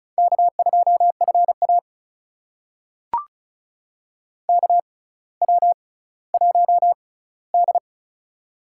Morse code, heard as a keyed tone, about 36 wpm, spaced as for 18 wpm, 700 Hz. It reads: K2FA KW1D
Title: USA Call Signs - 6x Character Spacing - 35wpm